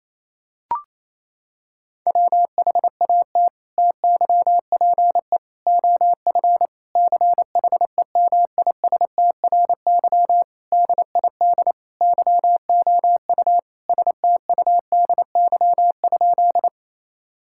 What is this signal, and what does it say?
Morse code, heard as a keyed tone, about 28 wpm, 700 Hz. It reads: WHAT TYPE OF CHEMISTRY DID YOU STUDY?
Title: Sentences from Top 1000 Words - Review Part 6 - 28wpm